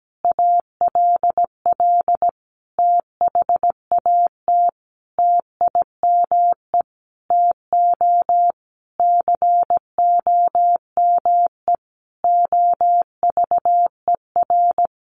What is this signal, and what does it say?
Morse code, heard as a keyed tone, about 17 wpm, 700 Hz. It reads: ALL THAT TIME TO COME OVER
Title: Sentences from Top 300 Words 17wpm